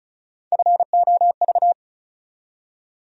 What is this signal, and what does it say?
Morse code, heard as a keyed tone, about 36 wpm, 700 Hz. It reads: FOV
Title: USA Call Sign Postfixes - 3 Letters 35wpm